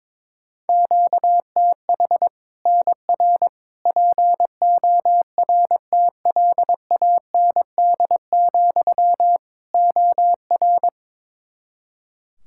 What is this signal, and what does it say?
Morse code, heard as a keyed tone, about 22 wpm, 700 Hz. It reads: QTH NR PORTLAND, OR
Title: Practice QSO - Exchange 1 - Beginning - 22wpm